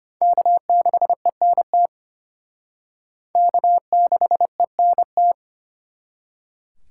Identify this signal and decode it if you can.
Morse code, heard as a keyed tone, about 31 wpm, 700 Hz. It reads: K6ENT K6ENT